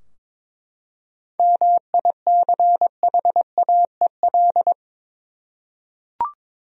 Morse code, heard as a keyed tone, about 22 wpm, 700 Hz. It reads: MICHAEL
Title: USA Common Names - 22wpm